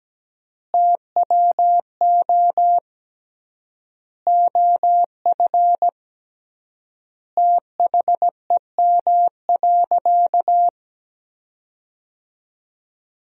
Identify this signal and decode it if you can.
Morse code, heard as a keyed tone, about 17 wpm, 700 Hz. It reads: TWO OF THEM.
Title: 3x Word-Spacing - Sentences from Top 100 Words - 17wpm